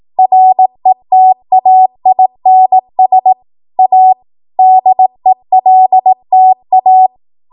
Morse code, heard as a keyed tone, about 18 wpm, 750 Hz. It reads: RETAINS A DELTA
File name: Morse / 18WPM